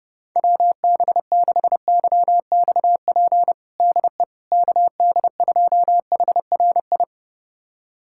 Morse code, heard as a keyed tone, about 32 wpm, 700 Hz. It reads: WB6YXP DE KD2HRI
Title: USA Callsign DE Callsign 30wpm